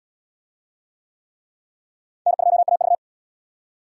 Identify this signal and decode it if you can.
Morse code, heard as a keyed tone, about 77 wpm, 700 Hz. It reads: N2AV